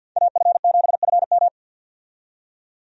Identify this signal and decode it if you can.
Morse code, heard as a keyed tone, about 53 wpm, 700 Hz. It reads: AF7FM
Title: USA Call Signs - 5 Characters - 50wpm